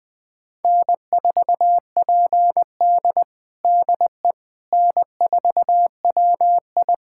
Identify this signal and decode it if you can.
Morse code, heard as a keyed tone, about 21 wpm, 700 Hz. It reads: N4PD DE N4WI
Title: USA Callsign DE Callsign - 4 Characters - 20wpm